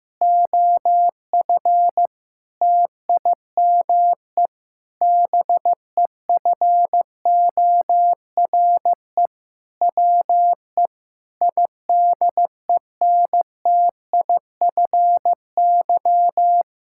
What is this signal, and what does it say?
Morse code, heard as a keyed tone, about 14 wpm, 700 Hz. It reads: OF TIME BEFORE WE IDENTIFY
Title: Sentences from Top 1000 Words - Review Part 6 - 15wpm